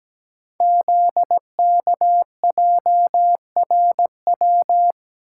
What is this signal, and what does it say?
Morse code, heard as a keyed tone, about 17 wpm, 700 Hz. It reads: ZKJRW
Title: Code Groups - 5 Letter:Numbers - 17wpm